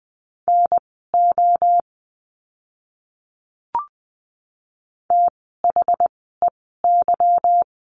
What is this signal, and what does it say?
Morse code, heard as a keyed tone, about 19 wpm, spaced as for 14 wpm, 700 Hz. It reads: NO THEY
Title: 146 Lesson 24f - Letter y - Words 20wpm